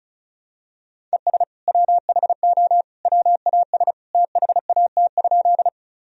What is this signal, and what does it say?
Morse code, heard as a keyed tone, about 36 wpm, 700 Hz. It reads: ES WHO WAS THAT?
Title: Sentences from Top 100 Words and Abbreviations 35wpm